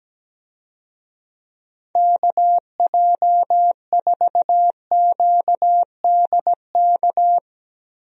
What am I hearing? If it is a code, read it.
Morse code, heard as a keyed tone, about 17 wpm, 700 Hz. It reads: KJ4QDK
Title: USA Call Signs - 6 Characters - Repeated 3x - 17wpm